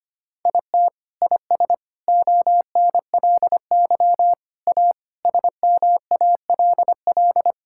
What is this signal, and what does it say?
Morse code, heard as a keyed tone, about 25 wpm, 700 Hz. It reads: IT IS ONLY A SMALL
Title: Sentences from Top 1000 Words - Review Part 5 - 25wpm